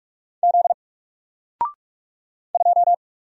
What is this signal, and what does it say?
Morse code, heard as a keyed tone, about 44 wpm, 700 Hz. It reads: Z 2
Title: Single Letter-Number Alt 45wpm